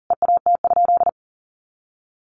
Morse code, heard as a keyed tone, about 40 wpm, 700 Hz. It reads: EAT?
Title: Sentences from Top 300 Words 40wpm